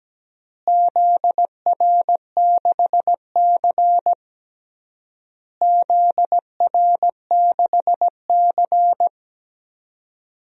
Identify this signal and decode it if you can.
Morse code, heard as a keyed tone, about 17 wpm, 700 Hz. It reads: ZR6C ZR6C
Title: DXCC-100 Call Signs - 4 Characters - Repeated 3x - 17wpm